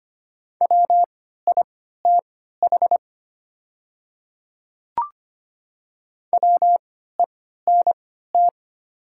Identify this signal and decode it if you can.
Morse code, heard as a keyed tone, about 25 wpm, spaced as for 14 wpm, 700 Hz. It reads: WITH WENT
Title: 3x Character Spacing - Top 200 Words - 25wpm